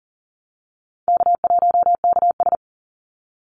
Morse code, heard as a keyed tone, about 40 wpm, 700 Hz. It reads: K1KS
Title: Easy Call Signs Repeated 3x - 40wpm